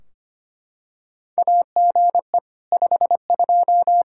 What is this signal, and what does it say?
Morse code, heard as a keyed tone, about 25 wpm, 700 Hz. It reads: AGE 52
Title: Practice QSO - Exchange 3 - More Info - 25wpm